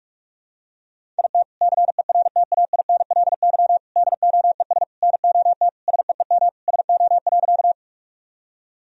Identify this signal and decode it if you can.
Morse code, heard as a keyed tone, about 42 wpm, 700 Hz. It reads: IT CERTAINLY DOES NOT SEEM SO.
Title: Sentences from Top 1000 Words - Review Part 1 - 45wpm